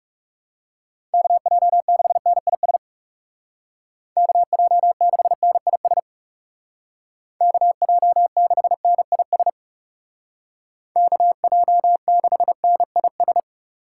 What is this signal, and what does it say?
Morse code, heard as a keyed tone, about 35 wpm, 700 Hz. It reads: KJ6NIS KJ6NIS KJ6NIS KJ6NIS
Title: Speed Racing - USA 2x3 Call Signs 30wpm